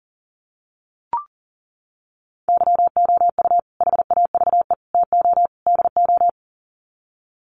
Morse code, heard as a keyed tone, about 40 wpm, 700 Hz. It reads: YOU HAVE TO DO
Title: Sets of 4 Words 40wpm